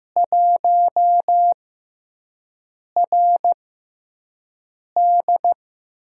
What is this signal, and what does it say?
Morse code, heard as a keyed tone, about 15 wpm, spaced as for 5 wpm, 700 Hz. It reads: 1RD
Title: USA Call Signs - 4 Characters - 6x Character Spacing - 15wpm